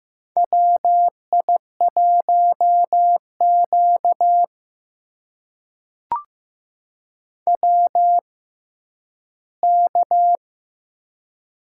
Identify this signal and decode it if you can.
Morse code, heard as a keyed tone, about 15 wpm, spaced as for 10 wpm, 700 Hz. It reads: WI1Q WK
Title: USA Call Signs - 4 Characters - 6x Character Spacing - 15wpm